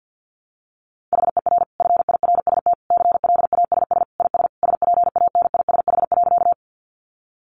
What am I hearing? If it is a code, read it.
Morse code, heard as a keyed tone, about 50 wpm, 700 Hz. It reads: HER FIRST CLASS IS SPANISH.
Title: Additional Easy Phrases 50wpm